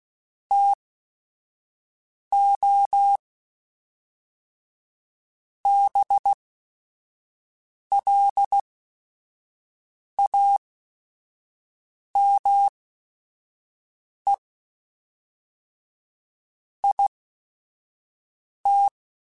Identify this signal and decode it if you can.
Morse code, heard as a keyed tone, about 16 wpm, spaced as for 5 wpm, 800 Hz. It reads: TO BLAME IT